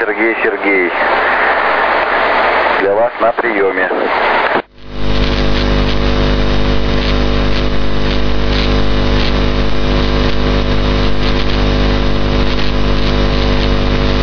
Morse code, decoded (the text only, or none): none